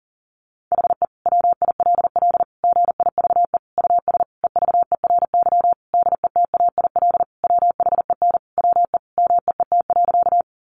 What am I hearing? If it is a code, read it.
Morse code, heard as a keyed tone, about 40 wpm, 700 Hz. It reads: HE WILL GIVE US EVERY DETAIL WHEN WE MEET.
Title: Sentences from Top 1500 Words 40wpm